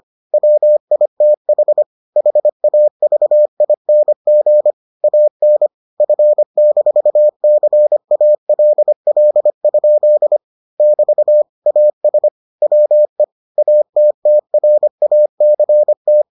Morse code, heard as a keyed tone, about 25 wpm, 600 Hz. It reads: WITH HAVING AN F-CALL? <BT> AS WE ATTRACT